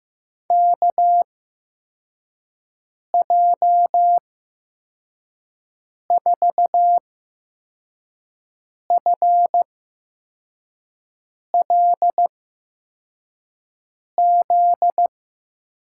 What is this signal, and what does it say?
Morse code, heard as a keyed tone, about 15 wpm, spaced as for 4 wpm, 700 Hz. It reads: KJ4FLZ
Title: USA Call Signs - 6 Characters - 8x Character Spacing - 15wpm